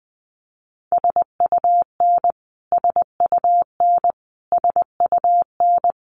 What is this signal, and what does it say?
Morse code, heard as a keyed tone, about 20 wpm, 700 Hz. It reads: SUN SUN SUN
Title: Top 500 Words Repeated 20wpm